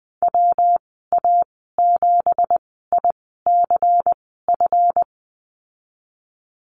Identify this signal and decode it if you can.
Morse code, heard as a keyed tone, about 20 wpm, spaced as for 15 wpm, 700 Hz. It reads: WA7ICF
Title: 173 Lesson 28g - Number 9 - Call Signs 20wpm